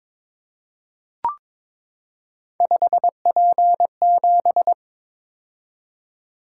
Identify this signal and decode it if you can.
Morse code, heard as a keyed tone, about 22 wpm, 700 Hz. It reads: 5P7
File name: practice / 22wpm / Three Letter-Number Combinations - 22wpm